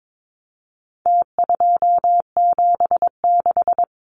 Morse code, heard as a keyed tone, about 22 wpm, 700 Hz. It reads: T276
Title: Numbers - Contest Style - 4-Digit 22wpm